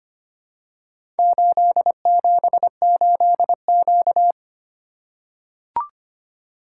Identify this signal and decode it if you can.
Morse code, heard as a keyed tone, about 25 wpm, 700 Hz. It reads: 878Q
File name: practice / 25wpm / Four Letter-Number Combinations - 25wpm